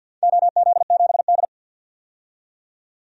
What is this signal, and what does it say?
Morse code, heard as a keyed tone, about 52 wpm, 700 Hz. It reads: OZ7D
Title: DXCC-100 Call Signs - 4 Characters - 50wpm